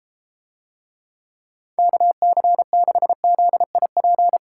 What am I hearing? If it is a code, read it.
Morse code, heard as a keyed tone, about 33 wpm, 700 Hz. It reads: KC6ZIP